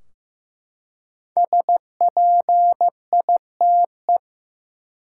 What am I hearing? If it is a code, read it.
Morse code, heard as a keyed tone, about 15 wpm, 700 Hz. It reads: SPITE
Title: Five Letter Words - 15wpm